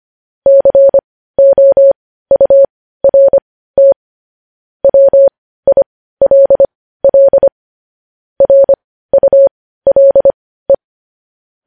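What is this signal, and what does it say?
Morse code, heard as a keyed tone, about 25 wpm, spaced as for 15 wpm, 550 Hz. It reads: COURT WILL RULE
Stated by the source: Morse code news headlines at 15.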